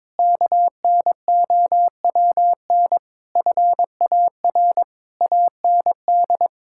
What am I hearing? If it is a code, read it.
Morse code, heard as a keyed tone, about 22 wpm, 700 Hz. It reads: KNOWN FAR AND
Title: Sentences from Top 1000 Words - Review Part 6 - 22wpm